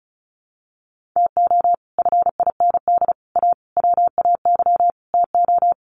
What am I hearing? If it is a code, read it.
Morse code, heard as a keyed tone, about 34 wpm, 700 Hz. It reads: TO FIND A WAY TO
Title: Sets of 5 Words 35wpm